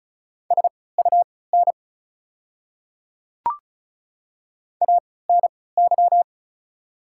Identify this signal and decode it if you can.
Morse code, heard as a keyed tone, about 36 wpm, spaced as for 20 wpm, 690 Hz. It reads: SUN ANY
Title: 3x Character Spacing - Top 500 Words - 35wpm